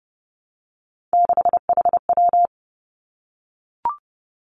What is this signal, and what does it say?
Morse code, heard as a keyed tone, about 30 wpm, 700 Hz. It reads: 6HW
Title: Three Challenging Letter-Number Combinations 30wpm